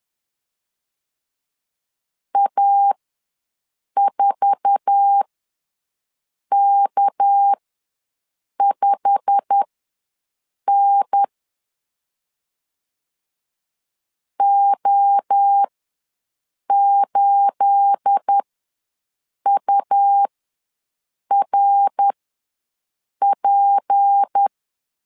Morse code, heard as a keyed tone, about 10 wpm, spaced as for 6 wpm, 800 Hz. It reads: A4K5N O8URP
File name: LEZIONI / 9 28 ws 10wpm 4124